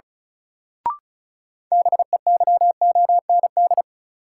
Morse code, heard as a keyed tone, about 36 wpm, 700 Hz. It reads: BEYOND
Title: Sentences from Top 1100 Words - New Word Drill Alt - 35wpm